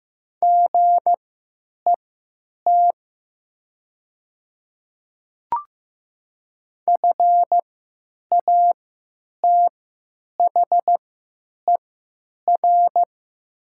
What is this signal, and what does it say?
Morse code, heard as a keyed tone, about 15 wpm, spaced as for 9 wpm, 700 Hz. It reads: GET FATHER
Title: 3x Character Spacing - Top 200 Words - 15wpm